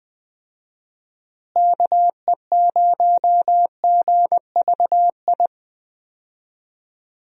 Morse code, heard as a keyed tone, about 20 wpm, 700 Hz. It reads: KE0GVI
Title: USA Call Signs - 6 Characters - Repeated 2x - 20wpm